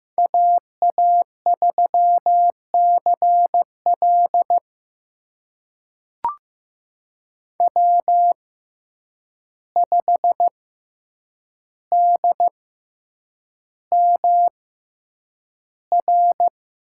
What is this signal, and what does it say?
Morse code, heard as a keyed tone, about 15 wpm, spaced as for 8 wpm, 700 Hz. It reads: AA3CL W5DMR
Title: USA Call Signs - 5 Characters - 6x Character Spacing - 15wpm